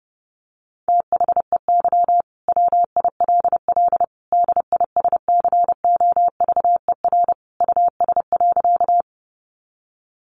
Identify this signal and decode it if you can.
Morse code, heard as a keyed tone, about 28 wpm, 700 Hz. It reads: THEY WILL DISCOVER US.
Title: Sentences from Top 1100 Words 30wpm